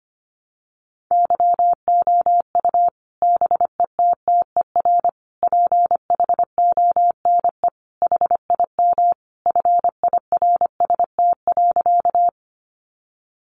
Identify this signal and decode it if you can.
Morse code, heard as a keyed tone, about 25 wpm, 700 Hz. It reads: YOU BETTER PHONE HIM FIRST.
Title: Sentences from Top 1100 Words 25wpm